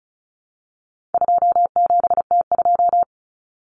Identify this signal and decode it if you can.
Morse code, heard as a keyed tone, about 34 wpm, 700 Hz. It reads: 27T2
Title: Common Cut Numbers - 4-Digit 35wpm